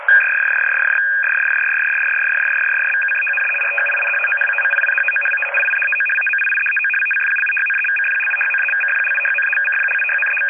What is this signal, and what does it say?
Two-tone teleprinter signal, about 850 Hz shift.